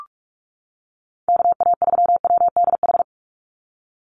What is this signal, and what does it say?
Morse code, heard as a keyed tone, about 44 wpm, 700 Hz. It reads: KA3WDH
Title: USA Call Signs 45wpm